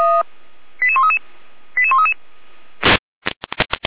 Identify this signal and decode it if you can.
Five-tone selective call (ZVEI), heard as quick runs of five tones, tones of about 70 ms. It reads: 88133 88133
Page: SDR/ZVEI